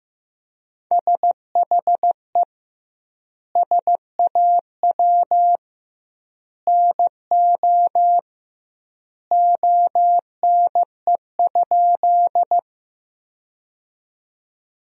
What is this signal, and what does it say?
Morse code, heard as a keyed tone, about 15 wpm, 700 Hz. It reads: SHE SAW NO ONE?
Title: 2x Word-Spacing - Sentences from Top 300 Words - 15wpm